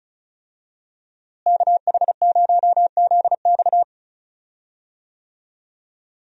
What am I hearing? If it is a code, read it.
Morse code, heard as a keyed tone, about 34 wpm, 700 Hz. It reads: KH0ZX
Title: USA Call Signs - 5 Characters - Repeated 3x - 35wpm